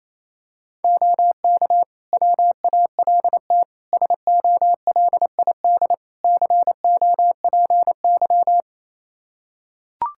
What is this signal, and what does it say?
Morse code, heard as a keyed tone, about 28 wpm, 700 Hz. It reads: OK WALT SOLID COPY
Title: Practice QSO - Exchange 3 - More Info - 28wpm